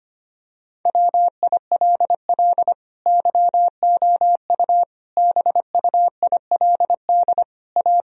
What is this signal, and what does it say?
Morse code, heard as a keyed tone, about 25 wpm, 700 Hz. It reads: WILL YOU BUILD A
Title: Sentences from Top 1000 Words - Review Part 1 - 25wpm